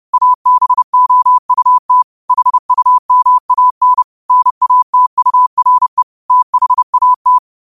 Morse code, heard as a keyed tone, about 30 wpm, 1000 Hz. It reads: ABOUT HUMAN NATURE THAT
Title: 30 WPM morse code quotes for Fri, 01 Mar 2024